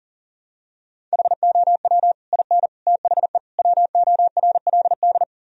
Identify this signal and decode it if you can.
Morse code, heard as a keyed tone, about 40 wpm, 700 Hz. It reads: HOW IN THE WORLD
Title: Sentences from Top 1000 Words - Review Part 4 - 40wpm